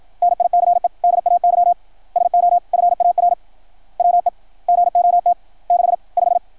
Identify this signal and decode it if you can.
Morse code, heard as a keyed tone, about 53 wpm, 660 Hz. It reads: NICE DAY SO FAR WE GOT 6 5
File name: archived / CRSnet